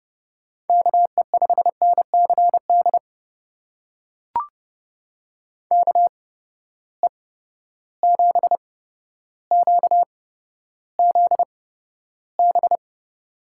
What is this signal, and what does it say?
Morse code, heard as a keyed tone, about 30 wpm, spaced as for 13 wpm, 700 Hz. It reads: KE5NCD KE7QZB
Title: USA Call Signs - 6 Characters - 8x Character Spacing - 30wpm